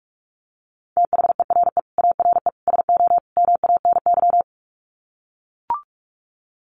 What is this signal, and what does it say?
Morse code, heard as a keyed tone, about 44 wpm, 700 Hz. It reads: THERE ARE SO MANY
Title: Sets of 4 Words 45wpm